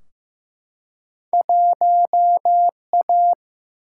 Morse code, heard as a keyed tone, about 15 wpm, 700 Hz. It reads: 1A
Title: Single-Letter Single-Number 15wpm